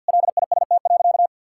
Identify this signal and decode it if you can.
Morse code, heard as a keyed tone, about 49 wpm, 700 Hz. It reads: LIST.
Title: Sentences from Top 1000 Words - Review Part 6 - 50wpm